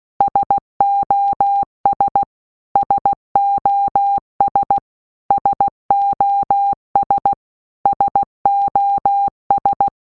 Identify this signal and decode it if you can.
Morse code, heard as a keyed tone, about 16 wpm, 800 Hz. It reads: SOS SOS SOS SOS
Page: SOS sound in Morse Code (Slow)